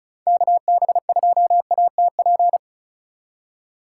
Morse code, heard as a keyed tone, about 36 wpm, 700 Hz. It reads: KB2ATP
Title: USA Call Signs - 6 Characters - 4x Character Spacing - 35wpm